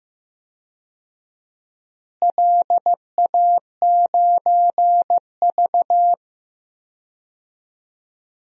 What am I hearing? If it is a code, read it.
Morse code, heard as a keyed tone, about 15 wpm, 700 Hz. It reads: LA9V
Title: DXCC-100 Call Signs - 4 Characters - Repeated 3x - 15wpm